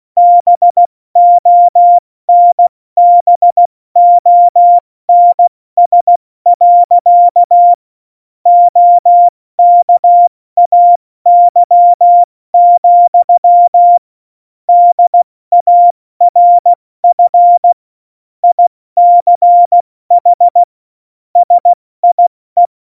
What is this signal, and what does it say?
Morse code, heard as a keyed tone, about 16 wpm, 700 Hz. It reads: BONBONS. OKAY, DARF ICH SIE
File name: Morsekurs / Witze / Witz 14 April 2020, 70 F80